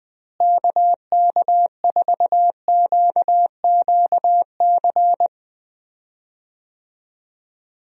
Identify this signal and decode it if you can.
Morse code, heard as a keyed tone, about 20 wpm, 700 Hz. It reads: KK4QQC
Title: USA Call Signs - 6 Characters - Repeated 3x - 20wpm